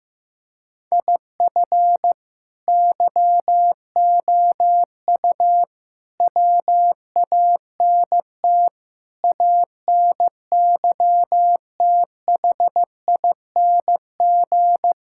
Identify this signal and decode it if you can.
Morse code, heard as a keyed tone, about 15 wpm, 700 Hz. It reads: IF YOU WANT ANYTHING
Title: Sentences from Top 1000 Words - Review Part 6 - 15wpm